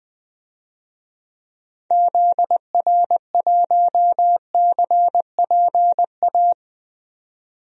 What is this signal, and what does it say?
Morse code, heard as a keyed tone, about 21 wpm, 700 Hz. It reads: ZR1CPA
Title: DXCC-100 Call Signs - 6 Characters - 20wpm